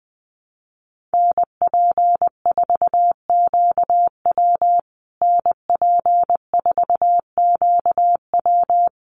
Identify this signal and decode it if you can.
Morse code, heard as a keyed tone, about 20 wpm, 700 Hz. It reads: NP4QW NP4QW
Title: USA Call Signs Repeated 3x 20wpm